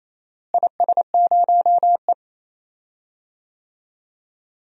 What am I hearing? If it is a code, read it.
Morse code, heard as a keyed tone, about 28 wpm, 700 Hz. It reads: IS0E
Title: DXCC-100 Call Signs - 4 Characters - Repeated 2x - 28wpm